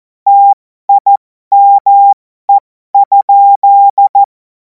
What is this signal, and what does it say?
Morse code, heard as a keyed tone, about 14 wpm, spaced as for 12 wpm, 800 Hz. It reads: TIME?
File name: Morsekurs / Witze